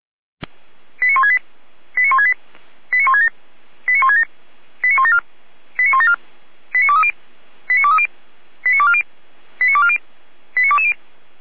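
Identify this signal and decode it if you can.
Five-tone selective call (ZVEI), heard as quick runs of five tones, tones of about 70 ms. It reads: 89168 89168 89167 89167 89174 89174 89230 89230 89240 89240 89220
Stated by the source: ZVEI five-tone selective call received by SDR, from the Feuerwehr Orlamünde (fire brigade) audio archive.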